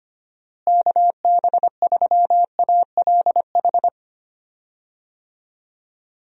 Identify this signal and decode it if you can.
Morse code, heard as a keyed tone, about 25 wpm, 700 Hz. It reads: KB3ALH
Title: USA Call Signs - 6 Characters - Repeated 3x - 25wpm